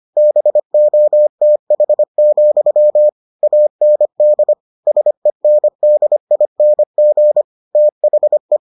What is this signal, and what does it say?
Morse code, heard as a keyed tone, about 25 wpm, 600 Hz. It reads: BOTH, AND SENDING THE